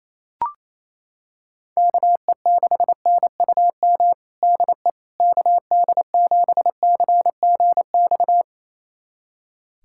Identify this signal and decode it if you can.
Morse code, heard as a keyed tone, about 27 wpm, 700 Hz. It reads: KE6NUM DE KD7CGX
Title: USA Callsign DE Callsign - 6 Characters - 28wpm